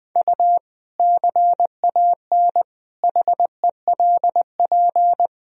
Morse code, heard as a keyed tone, about 20 wpm, 700 Hz. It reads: U CAN HELP
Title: Sentences from Top 200 Words and Abbreviations 20wpm